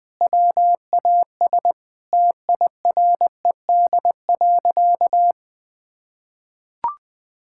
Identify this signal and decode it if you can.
Morse code, heard as a keyed tone, about 21 wpm, 700 Hz. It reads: WAS TIRED.
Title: Sentences with 3-Word Phrases - Spoken Beginning - 20wpm